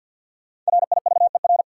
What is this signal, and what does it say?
Morse code, heard as a keyed tone, about 53 wpm, 700 Hz. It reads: RIVER
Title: Sentences from Top 300 Words - New Word Drill - 50wpm